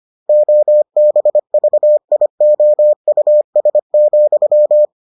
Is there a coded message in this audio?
Yes: Morse code from 0.3 to 4.8 s, heard as a keyed tone, about 25 wpm, 600 Hz. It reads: OBVIOUS,